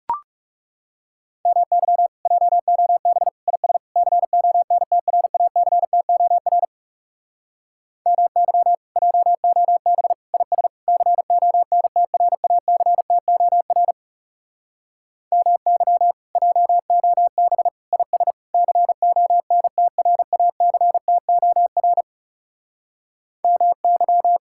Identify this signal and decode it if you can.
Morse code, heard as a keyed tone, about 40 wpm, 700 Hz. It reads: MY JOB IS CONTRACTOR MY JOB IS CONTRACTOR MY JOB IS CONTRACTOR MY